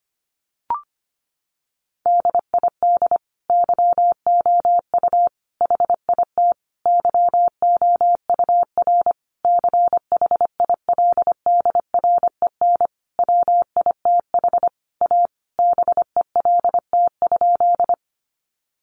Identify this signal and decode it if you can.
Morse code, heard as a keyed tone, about 25 wpm, 700 Hz. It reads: DID YOU HIT YOUR CHILDREN WITH A BELT?
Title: Sentences from Top 1400 Words 25wpm